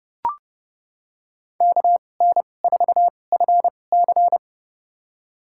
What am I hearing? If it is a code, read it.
Morse code, heard as a keyed tone, about 32 wpm, spaced as for 22 wpm, 700 Hz. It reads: KN4FC
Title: USA Call Signs - 2x Character Spacing - 30wpm